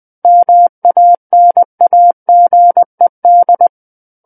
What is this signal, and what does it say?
Morse code, heard as a keyed tone, about 20 wpm, 700 Hz. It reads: MANAGED
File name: 20 WPM